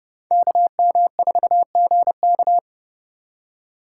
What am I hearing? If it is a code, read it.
Morse code, heard as a keyed tone, about 30 wpm, 700 Hz. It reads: KM4GK